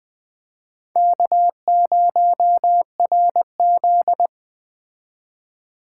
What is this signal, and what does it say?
Morse code, heard as a keyed tone, about 20 wpm, 700 Hz. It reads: K0RZ
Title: Speed Racing - USA 1x2 Call Signs 17wpm